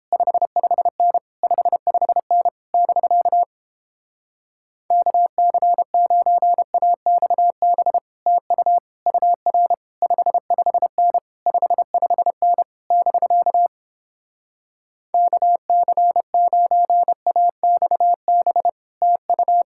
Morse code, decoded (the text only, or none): 55N 55N <BK> KC9AXB TU UR 55N 55N <BK> KC9AXB TU